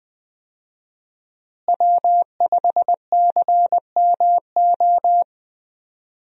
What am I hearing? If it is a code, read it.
Morse code, heard as a keyed tone, about 20 wpm, 700 Hz. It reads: W5CMO
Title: USA Call Signs - 5 Characters - Repeated 2x - 20wpm